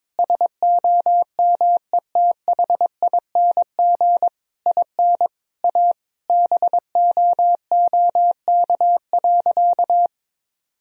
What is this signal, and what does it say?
Morse code, heard as a keyed tone, about 22 wpm, 700 Hz. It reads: SOMETHING IN A BOOK.
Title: Sentences from Top 1000 Words - Review Part 1 - 22wpm